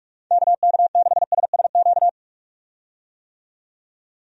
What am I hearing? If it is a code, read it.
Morse code, heard as a keyed tone, about 42 wpm, 700 Hz. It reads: KK6SSQ
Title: USA Call Signs - 6 Characters - 45wpm